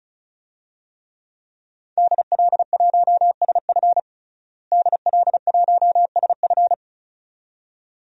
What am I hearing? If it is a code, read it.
Morse code, heard as a keyed tone, about 36 wpm, 700 Hz. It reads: DL1SF DL1SF
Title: DXCC-100 Call Signs - 5 Characters - Repeated 2x - 35wpm